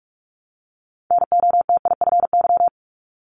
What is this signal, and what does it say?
Morse code, heard as a keyed tone, about 46 wpm, 700 Hz. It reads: NOTIFY
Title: Multi-letter Phonemes Repeated 3x 45wpm